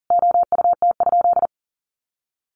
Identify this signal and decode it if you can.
Morse code, heard as a keyed tone, about 40 wpm, 700 Hz. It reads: OUT?
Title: Sentences from Top 300 Words 40wpm